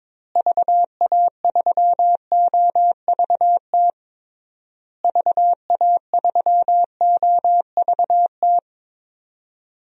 Morse code, heard as a keyed tone, about 21 wpm, 700 Hz. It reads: VA3OVT VA3OVT
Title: DXCC-100 Call Signs - 6 Characters - Repeated 3x - 22wpm